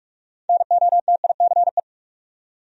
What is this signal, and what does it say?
Morse code, heard as a keyed tone, about 41 wpm, 700 Hz. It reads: NOTICE